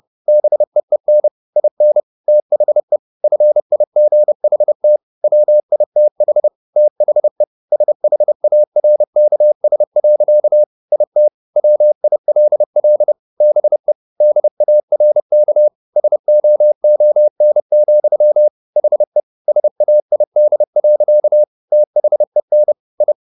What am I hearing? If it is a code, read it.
Morse code, heard as a keyed tone, about 30 wpm, 600 Hz. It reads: BEEN IN THE FIGHT WITH THE SHARKS. IT WILL BE DARK SOON, HE SAID. THEN I